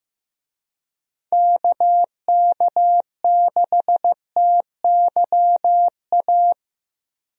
Morse code, heard as a keyed tone, about 16 wpm, 700 Hz. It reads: KK6TYA
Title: USA Call Signs - 6 Characters - Repeated 2x - 15wpm